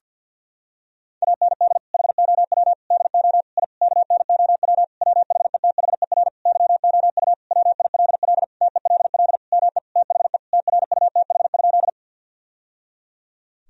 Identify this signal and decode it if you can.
Morse code, heard as a keyed tone, about 50 wpm, 700 Hz. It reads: AND HOW DO I KNOW WHETHER YOU WILL TELL ME THE TRUTH?
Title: Sentences from Top 1000 Words - Review Part 3 - 50wpm